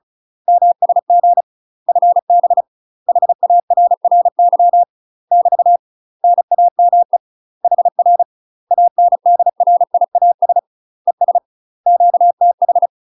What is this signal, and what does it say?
Morse code, heard as a keyed tone, about 36 wpm, 700 Hz. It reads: MSG FB HARRY <BT> NAME HR ANDRIAS ES QTH
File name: UK MASTER QSOs TRAINING / 35WPM / UK MASTER QSOs 0033